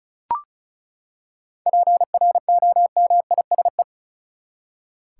Morse code, heard as a keyed tone, about 34 wpm, 700 Hz. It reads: PROMISE
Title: Seven Letter Words - 35wpm